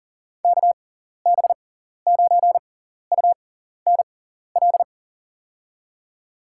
Morse code, heard as a keyed tone, about 40 wpm, spaced as for 14 wpm, 700 Hz. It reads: KB9UNL
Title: USA Call Signs - 6 Characters - 6x Character Spacing - 40wpm